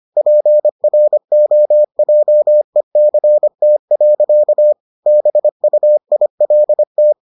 Morse code, heard as a keyed tone, about 25 wpm, 600 Hz. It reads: PROJECT. BUILT